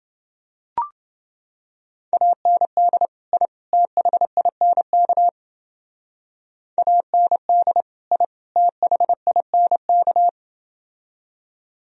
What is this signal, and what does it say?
Morse code, heard as a keyed tone, about 28 wpm, 700 Hz. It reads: AND I THINK AND I THINK